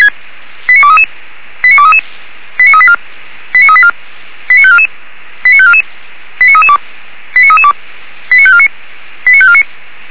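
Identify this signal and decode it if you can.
Five-tone selective call (ZVEI), heard as quick runs of five tones, tones of about 70 ms. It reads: 89233 89233 89384 89384 89544 89544 89332 89332 89549 89549
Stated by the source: Five-tone ZVEI selective call 89288, received on an SDR, from the Feuerwehr Orlamünde fire brigade audio server.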